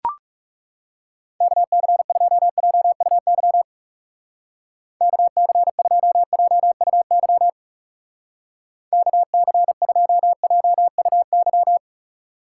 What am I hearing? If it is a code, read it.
Morse code, heard as a keyed tone, about 37 wpm, 700 Hz. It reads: KC2JUY KC2JUY KC2JUY
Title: Speed Racing - USA 2x3 Call Signs 30wpm